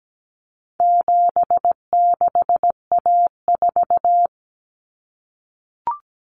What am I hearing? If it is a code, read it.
Morse code, heard as a keyed tone, about 17 wpm, 700 Hz. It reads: 76A4
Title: Common Cut Numbers - 4-Digit 17wpm